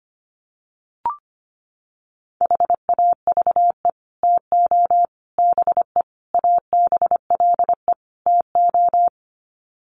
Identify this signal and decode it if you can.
Morse code, heard as a keyed tone, about 25 wpm, 700 Hz. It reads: HAVE TO BE ABLE TO
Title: Sets of 5 Words 25wpm